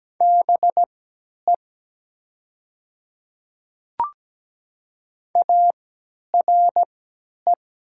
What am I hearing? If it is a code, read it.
Morse code, heard as a keyed tone, about 17 wpm, spaced as for 10 wpm, 700 Hz. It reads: BE ARE
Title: 3x Character Spacing - Top 100 Words - 17wpm